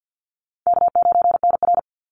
Morse code, heard as a keyed tone, about 48 wpm, 700 Hz. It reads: K9NR
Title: Easy Call Signs Repeated 3x - 50wpm